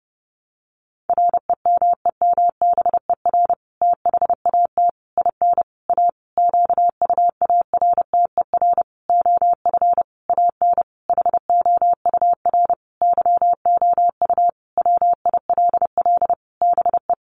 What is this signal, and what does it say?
Morse code, heard as a keyed tone, about 30 wpm, 700 Hz. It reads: REMEMBER THAT IN A QUARTER OF AN HOUR YOU WILL BE
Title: Sentences from Top 1300 Words 30wpm